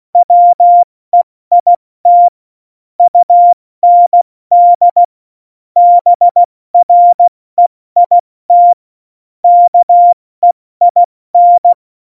Morse code, heard as a keyed tone, about 16 wpm, 700 Hz. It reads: WEIT UND BREIT KEIN
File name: Morsekurs / Witze / Witz 31. Oktober 17, Tempo 70